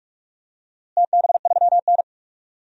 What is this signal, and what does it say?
Morse code, heard as a keyed tone, about 45 wpm, 700 Hz. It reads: TB3N
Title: DXCC-100 Call Signs - 4 Characters - Repeated 2x - 45wpm